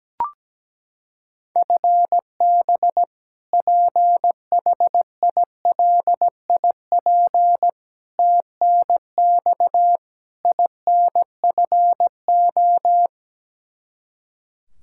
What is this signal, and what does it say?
Morse code, heard as a keyed tone, about 17 wpm, 700 Hz. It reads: FB PHILIP TNX INFO
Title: Practice QSO - Exchange 3 - More Info - 17wpm